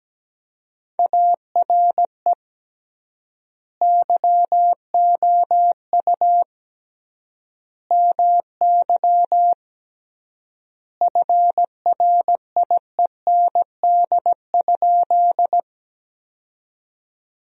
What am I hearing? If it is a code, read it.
Morse code, heard as a keyed tone, about 17 wpm, 700 Hz. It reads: ARE YOU MY FRIEND?
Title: 3x Word-Spacing - Sentences from Top 300 Words - 17wpm